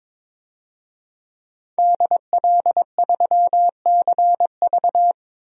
Morse code, heard as a keyed tone, about 21 wpm, 700 Hz. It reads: DL3CV
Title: DXCC-100 Call Signs - 5 Characters - 22wpm